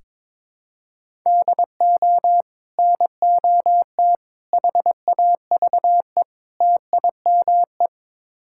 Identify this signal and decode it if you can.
Morse code, heard as a keyed tone, about 22 wpm, 700 Hz. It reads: DO NOT HAVE TIME
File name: practice / 22wpm / Build A Phrase - 5-Word Phrases - 22wpm